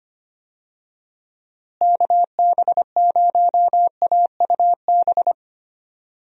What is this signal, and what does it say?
Morse code, heard as a keyed tone, about 25 wpm, 700 Hz. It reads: KB0AUB
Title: USA Call Signs - 6 Characters - Repeated 3x - 25wpm